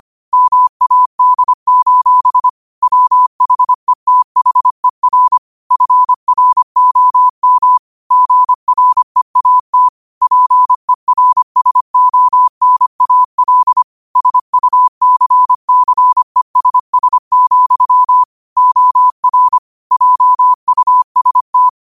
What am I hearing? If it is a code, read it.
Morse code, heard as a keyed tone, about 25 wpm, 1000 Hz. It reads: MAD: WHETHER FROM GREAT PERSONAL SUCCESS, OR JUST